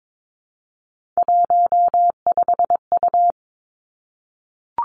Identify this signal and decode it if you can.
Morse code, heard as a keyed tone, about 22 wpm, 700 Hz. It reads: 15U
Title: Three Challenging Letter-Number Combinations 22wpm